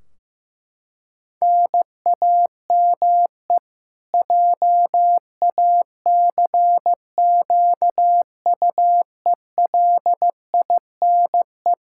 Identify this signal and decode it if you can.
Morse code, heard as a keyed tone, about 15 wpm, 700 Hz. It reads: NAME JACQUELINE
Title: Practice QSO - Exchange 1 - Beginning - 15wpm